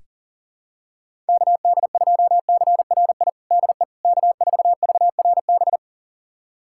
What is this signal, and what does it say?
Morse code, heard as a keyed tone, about 40 wpm, 700 Hz. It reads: KD2CRI DE K4VRB
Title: USA Callsign DE Callsign 40wpm